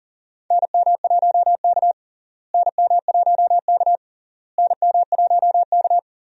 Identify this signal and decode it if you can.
Morse code, heard as a keyed tone, about 40 wpm, 700 Hz. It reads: NM1K NM1K NM1K
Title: USA Call Signs - 4 Characters - Repeated 3x - 40wpm